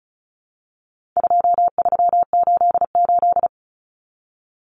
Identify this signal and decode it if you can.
Morse code, heard as a keyed tone, about 34 wpm, 700 Hz. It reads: 2388
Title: Numbers - 4-Digit 35wpm